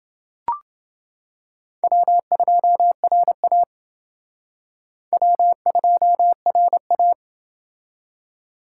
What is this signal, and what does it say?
Morse code, heard as a keyed tone, about 29 wpm, 700 Hz. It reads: W2RA W2RA